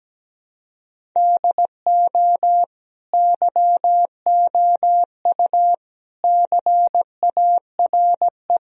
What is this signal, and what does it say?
Morse code, heard as a keyed tone, about 17 wpm, 700 Hz. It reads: DO YOU CARE
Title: Sentences from Top 1000 Words - Review Part 1 - 17wpm